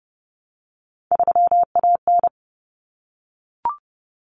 Morse code, heard as a keyed tone, about 30 wpm, 700 Hz. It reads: 3AN
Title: Common Cut Numbers - 3-Digit 30wpm